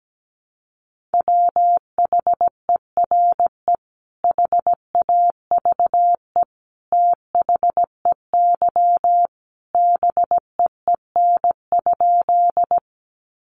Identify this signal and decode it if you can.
Morse code, heard as a keyed tone, about 17 wpm, 700 Hz. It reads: WHERE HAVE THEY BEEN?
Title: Sentences from Top 200 Words 17wpm